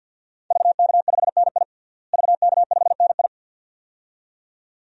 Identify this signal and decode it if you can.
Morse code, heard as a keyed tone, about 51 wpm, 700 Hz. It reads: VK5NI VK5NI
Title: DXCC-100 Call Signs - 5 Characters - Repeated 2x - 50wpm